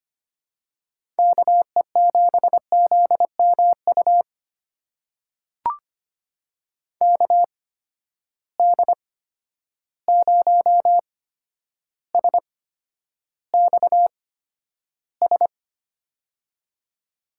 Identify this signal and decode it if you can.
Morse code, heard as a keyed tone, about 25 wpm, spaced as for 11 wpm, 700 Hz. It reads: KE7ZMU KD0SXS
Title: USA Call Signs - 6 Characters - 8x Character Spacing - 25wpm